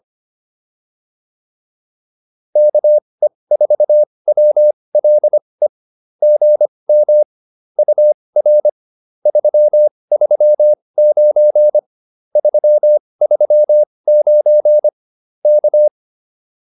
Morse code, heard as a keyed tone, about 25 wpm, spaced as for 20 wpm, 600 Hz. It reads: KE4WLE GM UR 339 339 K